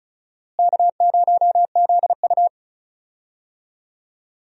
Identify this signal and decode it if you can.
Morse code, heard as a keyed tone, about 34 wpm, 700 Hz. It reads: K0ZU
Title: USA Call Signs - 4 Characters - Repeated 2x - 35wpm